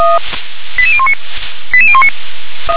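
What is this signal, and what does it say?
Five-tone selective call (ZVEI), heard as quick runs of five tones, tones of about 70 ms. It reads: 88019 88019
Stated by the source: ZVEI five-tone call received by SDR.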